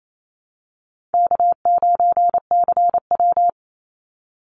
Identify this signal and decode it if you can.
Morse code, heard as a keyed tone, about 28 wpm, 700 Hz. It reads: K9CW
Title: Easy Call Signs - 28wpm